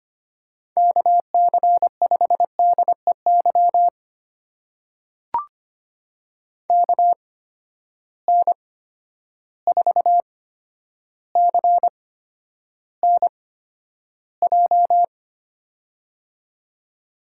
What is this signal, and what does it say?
Morse code, heard as a keyed tone, about 25 wpm, spaced as for 11 wpm, 700 Hz. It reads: KC5DEY KN4CNJ
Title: USA Call Signs - 8x Character Spacing - 25wpm